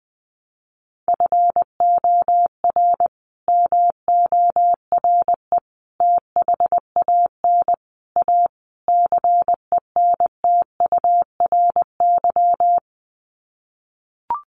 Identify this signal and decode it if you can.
Morse code, heard as a keyed tone, about 20 wpm, 700 Hz. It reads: FOR MORE THAN A CENTURY
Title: Sets of 5 Words Encore2 20wpm